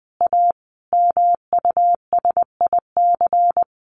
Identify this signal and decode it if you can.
Morse code, heard as a keyed tone, about 20 wpm, 700 Hz. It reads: A MUSIC
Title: Sentences from Top 700 Words 20wpm